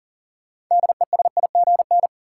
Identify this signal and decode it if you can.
Morse code, heard as a keyed tone, about 37 wpm, 700 Hz. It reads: DESIGN
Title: Sentences from Top 800 Words - New Word Drill - 40wpm